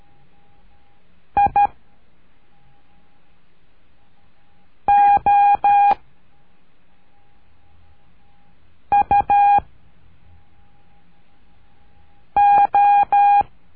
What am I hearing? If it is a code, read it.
Morse code, heard as a keyed tone, about 13 wpm, 810 Hz. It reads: I O U O
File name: cursos / CURSO DE TELEGRAFIA / Telegrafia - Trilha 01